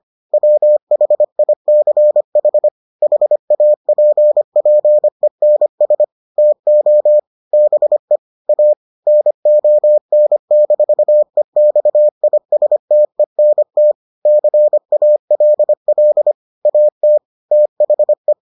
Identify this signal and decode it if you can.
Morse code, heard as a keyed tone, about 25 wpm, 600 Hz. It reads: WHICH HAPPENS TO BE A NON-EXISTENT CALL AT THE